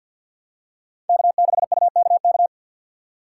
Morse code, heard as a keyed tone, about 50 wpm, 700 Hz. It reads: K6UKK
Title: USA Call Signs - 5 Characters - 4x Character Spacing - 50wpm